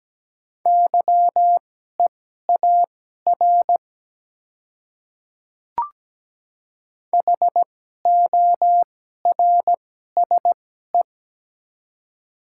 Morse code, heard as a keyed tone, about 17 wpm, spaced as for 12 wpm, 700 Hz. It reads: YEAR HORSE
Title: 2x Character Spacing - Top 300 Words - 17wpm